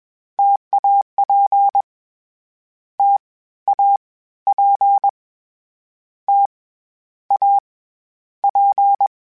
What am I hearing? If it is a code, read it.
Morse code, heard as a keyed tone, about 21 wpm, 800 Hz. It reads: TAP T A P T A P
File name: Difficult Combos / Difficult Combos-mp3s / T AOEI @20